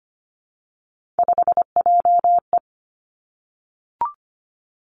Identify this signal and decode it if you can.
Morse code, heard as a keyed tone, about 25 wpm, 700 Hz. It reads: 5JE
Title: Three Challenging Letter-Number Combinations 25wpm